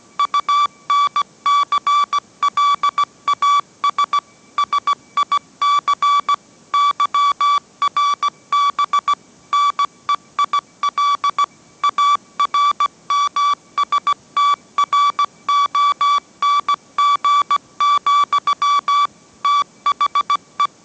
Morse code, heard as a keyed tone, about 17 wpm, 1150 Hz. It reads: UNCLAS SIC YRB NEIL ARMSTRONG, THE
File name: mrx / mrx080 18wpm